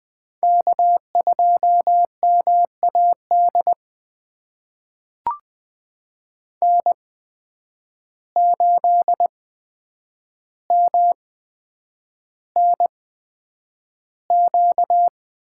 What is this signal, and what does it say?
Morse code, heard as a keyed tone, about 20 wpm, spaced as for 9 wpm, 700 Hz. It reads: K2MAD N8MNQ
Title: USA Call Signs - 5 Characters - 8x Character Spacing - 20wpm